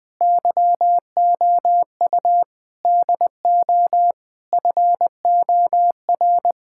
Morse code, heard as a keyed tone, about 21 wpm, 700 Hz. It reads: YOU DO FOR
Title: Sentences with 4-Word Phrases - 20wpm